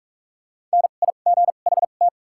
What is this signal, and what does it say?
Morse code, heard as a keyed tone, about 44 wpm, spaced as for 32 wpm, 700 Hz. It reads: NIGHT
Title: 2x Character Spacing - Top 500 Words - 45wpm